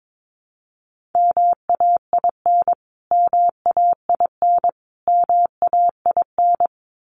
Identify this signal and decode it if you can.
Morse code, heard as a keyed tone, about 22 wpm, 700 Hz. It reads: MAIN MAIN MAIN
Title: Top 300 Words Repeated 22wpm